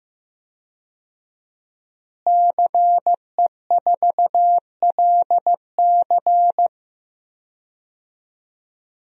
Morse code, heard as a keyed tone, about 15 wpm, 700 Hz. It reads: CE4LC
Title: DXCC-100 Call Signs - 5 Characters - Repeated 2x - 15wpm